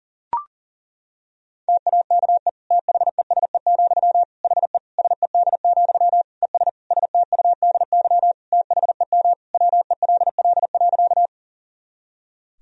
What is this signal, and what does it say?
Morse code, heard as a keyed tone, about 40 wpm, 700 Hz. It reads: TAKE THESE, HE SED, ES STUDY THEM WELL.
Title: Sentences from Top 300 Words and Abbreviations 40wpm